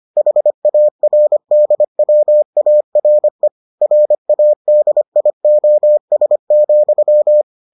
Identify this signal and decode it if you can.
Morse code, heard as a keyed tone, about 25 wpm, 600 Hz. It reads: HARDWARE RADIOS,